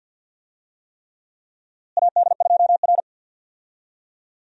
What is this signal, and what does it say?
Morse code, heard as a keyed tone, about 52 wpm, 700 Hz. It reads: AD2R
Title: USA Call Signs - 4 Characters - Repeated 3x - 50wpm